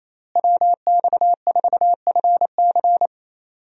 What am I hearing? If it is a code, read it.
Morse code, heard as a keyed tone, about 28 wpm, 700 Hz. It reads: WX4FC
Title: USA Call Signs - 5 Characters - Repeated 3x - 28wpm